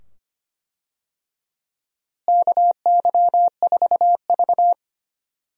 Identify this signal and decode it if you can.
Morse code, heard as a keyed tone, about 25 wpm, 700 Hz. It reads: KY4V
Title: USA Call Signs - 4 Characters - Repeated 2x - 25wpm